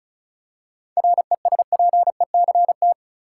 Morse code, heard as a keyed tone, about 34 wpm, 700 Hz. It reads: RESPECT
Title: Seven Letter Words - 35wpm